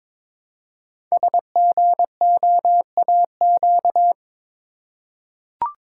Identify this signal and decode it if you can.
Morse code, heard as a keyed tone, about 22 wpm, 700 Hz. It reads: SGOAQ
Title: Code Groups - 5 Letters - 22wpm